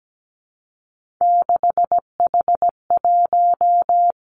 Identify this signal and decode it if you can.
Morse code, heard as a keyed tone, about 17 wpm, 700 Hz. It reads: 6H1
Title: Three Challenging Letter-Number Combinations 17wpm